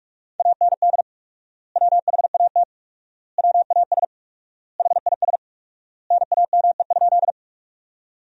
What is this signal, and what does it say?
Morse code, heard as a keyed tone, about 43 wpm, 700 Hz. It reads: AND WHAT WAS HIS NAME?
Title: 4x Word-Spacing - Sentences from Top 200 Words - 45wpm